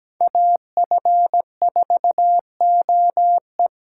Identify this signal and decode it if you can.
Morse code, heard as a keyed tone, about 17 wpm, 700 Hz. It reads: AF4OE
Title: USA Call Signs - 5 Characters - 6x Character Spacing - 17wpm